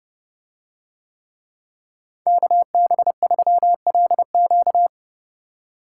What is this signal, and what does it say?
Morse code, heard as a keyed tone, about 29 wpm, 700 Hz. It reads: KB3LQ